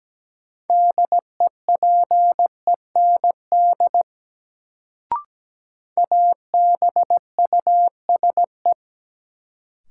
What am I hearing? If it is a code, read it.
Morse code, heard as a keyed tone, about 17 wpm, 700 Hz. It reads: DEPEND ABUSE
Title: Sentences from Top 1300 Words - New Word Drill Alt - 17wpm